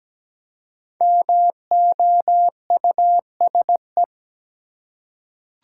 Morse code, heard as a keyed tone, about 17 wpm, 700 Hz. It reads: MOUSE
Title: Five Letter Words - 17wpm